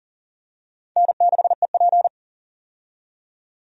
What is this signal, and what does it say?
Morse code, heard as a keyed tone, about 40 wpm, 700 Hz. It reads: N6EP
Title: USA Call Signs - 4 Characters - 40wpm